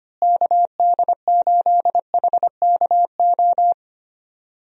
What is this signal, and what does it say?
Morse code, heard as a keyed tone, about 25 wpm, 700 Hz. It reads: KD8HKO
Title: USA Call Signs - 6 Characters - Repeated 3x - 25wpm